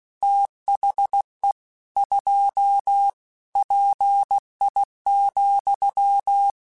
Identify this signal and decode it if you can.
Morse code, heard as a keyed tone, about 16 wpm, 800 Hz. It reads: THE 2 PI,
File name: Technical Subjects / Technical Subjects-mp3s / XL @15